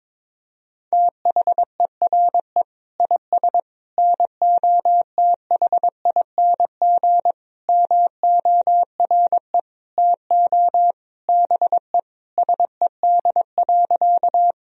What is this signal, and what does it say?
Morse code, heard as a keyed tone, about 22 wpm, 700 Hz. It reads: THERE IS NOTHING MORE TO BE SED.
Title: Sentences from Top 500 Words and Abbreviations 22wpm